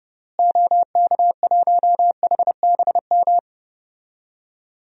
Morse code, heard as a keyed tone, about 30 wpm, 700 Hz. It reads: OK1HBM
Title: DXCC-100 Call Signs - 6 Characters - Repeated 2x - 30wpm